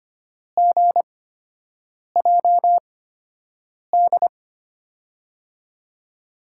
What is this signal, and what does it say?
Morse code, heard as a keyed tone, about 25 wpm, spaced as for 7 wpm, 700 Hz. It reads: GJD
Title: USA Call Signs - 6 Characters - 8x Character Spacing - 25wpm